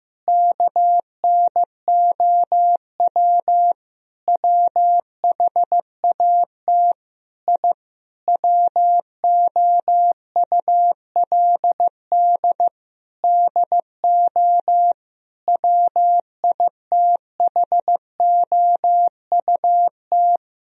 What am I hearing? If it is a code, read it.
Morse code, heard as a keyed tone, about 15 wpm, 700 Hz. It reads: KNOW WHAT I WOULD DO WITHOUT
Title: Sentences from Top 1000 Words - Review Part 3 - 15wpm